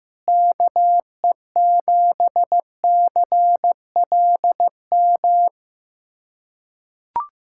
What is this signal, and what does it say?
Morse code, heard as a keyed tone, about 15 wpm, 700 Hz. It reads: KE7CLM
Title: USA Callsign DE Callsign - 6 Characters - 15wpm